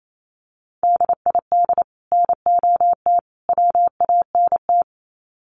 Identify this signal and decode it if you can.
Morse code, heard as a keyed tone, about 28 wpm, 700 Hz. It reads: DID NOT WANT
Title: Sets of 3 Words 28wpm